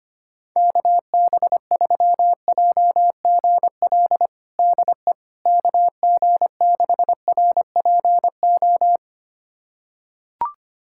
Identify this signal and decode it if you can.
Morse code, heard as a keyed tone, about 25 wpm, 700 Hz. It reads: KB3JGL DE KG6RPO
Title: USA Callsign DE Callsign 25wpm